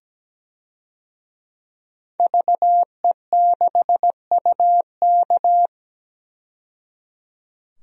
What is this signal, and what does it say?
Morse code, heard as a keyed tone, about 17 wpm, 700 Hz. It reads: VE6UK
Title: DXCC-100 Call Signs - 5 Characters - 17wpm